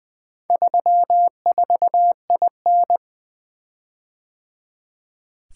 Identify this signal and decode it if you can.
Morse code, heard as a keyed tone, about 24 wpm, 700 Hz. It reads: 34IN
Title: Speed-Racing - Four Letter-Number Combinations - 20wpm